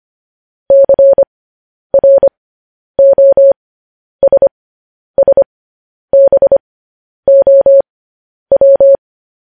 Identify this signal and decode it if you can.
Morse code, heard as a keyed tone, about 25 wpm, spaced as for 10 wpm, 550 Hz. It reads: CROSSBOW
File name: podcast / News headlines-1x-25x10